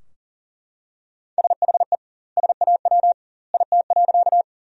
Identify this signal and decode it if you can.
Morse code, heard as a keyed tone, about 37 wpm, 700 Hz. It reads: SHE SAW IT.
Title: 2x Word-Spacing - Sentences from Top 300 Words - 40wpm